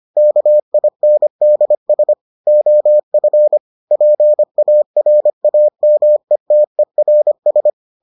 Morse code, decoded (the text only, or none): KINDS OF PARAMETERS